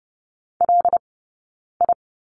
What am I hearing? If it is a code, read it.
Morse code, heard as a keyed tone, about 30 wpm, spaced as for 9 wpm, 690 Hz. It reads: LI
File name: Lesson 17d - Letter u - Two Characters 30wpm